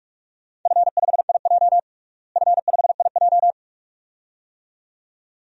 Morse code, heard as a keyed tone, about 46 wpm, 700 Hz. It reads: F5IJ F5IJ
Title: DXCC-100 Call Signs - 4 Characters - Repeated 2x - 45wpm